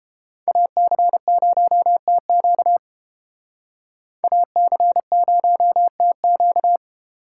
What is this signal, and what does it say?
Morse code, heard as a keyed tone, about 32 wpm, 700 Hz. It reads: AC0TQ AC0TQ